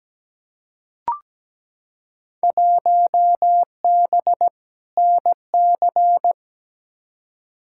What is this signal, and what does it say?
Morse code, heard as a keyed tone, about 17 wpm, 700 Hz. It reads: 1B NC
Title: Field Day Exchanges Encore - 17wpm